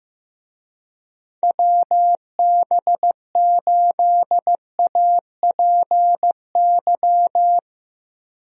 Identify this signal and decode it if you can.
Morse code, heard as a keyed tone, about 15 wpm, 700 Hz. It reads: WB8APY
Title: USA Call Signs - 6 Characters - Repeated 2x - 15wpm